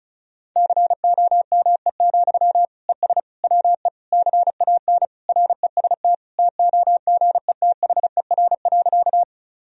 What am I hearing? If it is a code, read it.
Morse code, heard as a keyed tone, about 34 wpm, 700 Hz. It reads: COME, ES WE CAN REST TOGETHER.
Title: Sentences from Top 500 Words and Abbreviations 35wpm